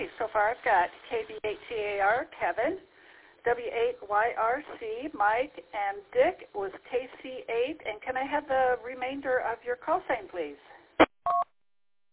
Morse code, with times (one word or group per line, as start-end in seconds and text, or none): none